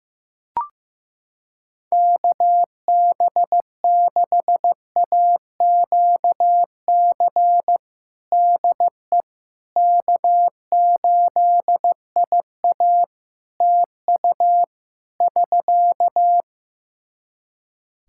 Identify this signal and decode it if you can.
Morse code, heard as a keyed tone, about 15 wpm, 700 Hz. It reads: KB6AQC DE K8IA TU <SK>
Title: K1USN SST QSO Practice 15wpm